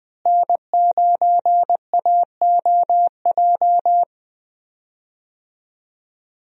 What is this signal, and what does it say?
Morse code, heard as a keyed tone, about 20 wpm, 700 Hz. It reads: N9AOJ
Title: USA Call Signs - 5 Characters - Repeated 3x - 20wpm